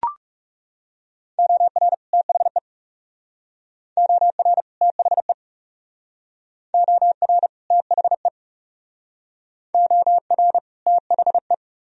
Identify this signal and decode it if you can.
Morse code, heard as a keyed tone, about 35 wpm, 700 Hz. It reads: OR THE OR THE OR THE OR THE
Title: Speed-Racing - Sets of 2 Words 30wpm